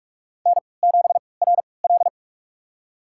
Morse code, heard as a keyed tone, about 42 wpm, spaced as for 25 wpm, 700 Hz. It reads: N7RL
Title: USA Call Signs - 4 Characters - 3x Character Spacing - 45wpm